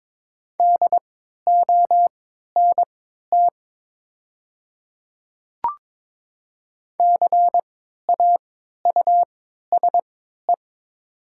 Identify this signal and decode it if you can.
Morse code, heard as a keyed tone, about 22 wpm, spaced as for 12 wpm, 700 Hz. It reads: DONT CAUSE
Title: 3x Character Spacing - Top 300 Words - 22wpm